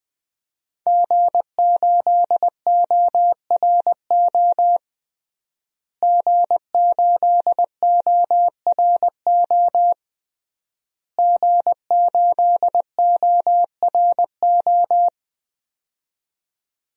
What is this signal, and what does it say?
Morse code, heard as a keyed tone, about 20 wpm, 700 Hz. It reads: G8ORO G8ORO G8ORO
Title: DXCC-100 Call Signs - 5 Characters - Repeated 3x - 20wpm